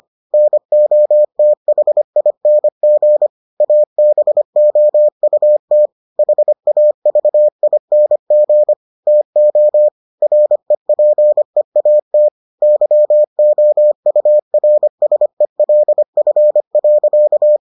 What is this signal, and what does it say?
Morse code, heard as a keyed tone, about 25 wpm, 600 Hz. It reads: NOTHING ABOUT HAVING TO REPEAT YOURSELF.